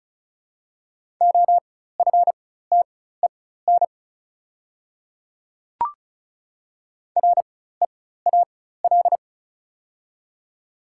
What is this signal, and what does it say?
Morse code, heard as a keyed tone, about 34 wpm, spaced as for 16 wpm, 690 Hz. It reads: OFTEN REAL
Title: 4x Character Spacing - Top 300 Words - 35wpm